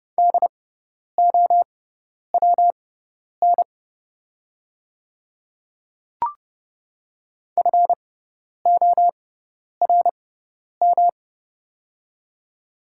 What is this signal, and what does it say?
Morse code, heard as a keyed tone, about 30 wpm, spaced as for 10 wpm, 700 Hz. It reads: DOWN FORM
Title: 6x Character Spacing - Top 200 Words - 30wpm